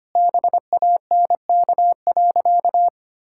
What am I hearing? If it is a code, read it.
Morse code, heard as a keyed tone, about 25 wpm, 700 Hz. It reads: BANK.
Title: Sentences from Top 1000 Words - Review Part 6 - 25wpm